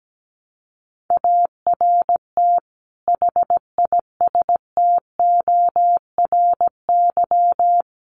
Morse code, heard as a keyed tone, about 17 wpm, 700 Hz. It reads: ART HISTORY
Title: Sentences from Top 600 Words 17wpm